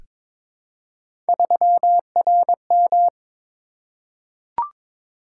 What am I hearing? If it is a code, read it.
Morse code, heard as a keyed tone, about 22 wpm, 700 Hz. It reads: 3RM
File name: practice / 22wpm / Three Letter-Number Combinations - 22wpm